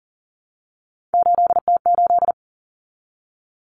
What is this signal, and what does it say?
Morse code, heard as a keyed tone, about 40 wpm, 700 Hz. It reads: 8T8
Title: Common Cut Numbers - 3-Digit 40wpm